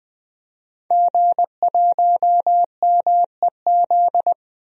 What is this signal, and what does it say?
Morse code, heard as a keyed tone, about 20 wpm, 700 Hz. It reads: G1MEZ
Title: DXCC-100 Call Signs - 5 Characters - Repeated 2x - 20wpm